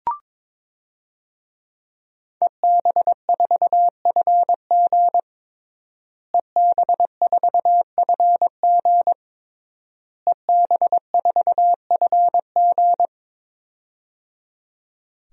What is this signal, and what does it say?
Morse code, heard as a keyed tone, about 22 wpm, 700 Hz. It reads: EB4FG EB4FG EB4FG
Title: DXCC-100 Call Signs - 5 Characters - Repeated 3x - 22wpm